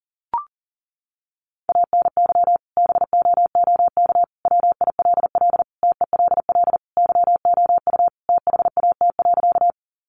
Morse code, heard as a keyed tone, about 40 wpm, 700 Hz. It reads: ANY BOOK WILL TELL YOU THAT.
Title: Sentences from Top 300 Words 40wpm